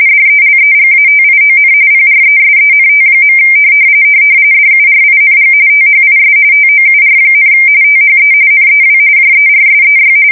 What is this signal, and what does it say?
RTTY, about 45.45 baud. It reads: DELEGATION PRESENT. DURING THIS TOUR, THE NEW JERSEY
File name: fsk170 / battleshipnj